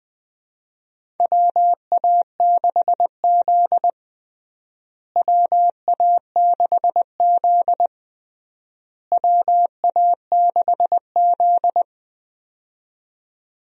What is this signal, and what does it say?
Morse code, heard as a keyed tone, about 21 wpm, 700 Hz. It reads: WA6Z WA6Z WA6Z
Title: USA Call Signs - 4 Characters - Repeated 3x - 20wpm